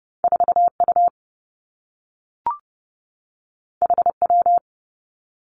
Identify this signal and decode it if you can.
Morse code, heard as a keyed tone, about 30 wpm, 700 Hz. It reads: 4 U H W
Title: Two Challenging Letter-Number Combinations 30wpm